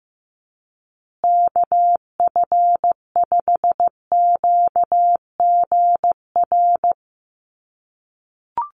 Morse code, heard as a keyed tone, about 15 wpm, 700 Hz. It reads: KF5QGR
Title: USA Call Signs Repeated 3x 15wpm